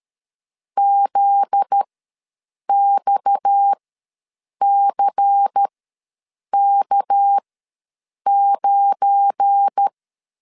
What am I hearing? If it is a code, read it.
Morse code, heard as a keyed tone, about 13 wpm, spaced as for 7 wpm, 800 Hz. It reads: ZXCK9
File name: 7 GF9Z ws 12wpm